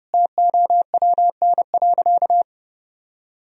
Morse code, heard as a keyed tone, about 30 wpm, 700 Hz. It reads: TOWN.
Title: Sentences from Top 1000 Words - Review Part 2 - 30wpm